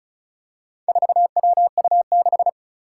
Morse code, heard as a keyed tone, about 36 wpm, 700 Hz. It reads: 4WU6
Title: Four Challenging Letter-Number Combinations 35wpm